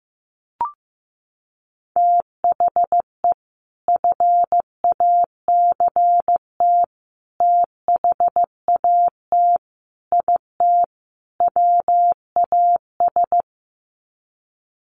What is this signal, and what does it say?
Morse code, heard as a keyed tone, about 15 wpm, 700 Hz. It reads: THE FACT THAT IT WAS
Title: Sets of 5 Words Encore2 15wpm